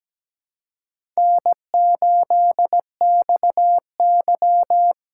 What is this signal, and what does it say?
Morse code, heard as a keyed tone, about 17 wpm, 700 Hz. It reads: N8XY
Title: USA Call Signs - 4 Characters - 4x Character Spacing - 17wpm